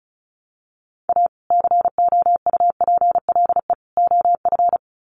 Morse code, heard as a keyed tone, about 34 wpm, 700 Hz. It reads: A COUPLE OF
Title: Sets of 3 Words 35wpm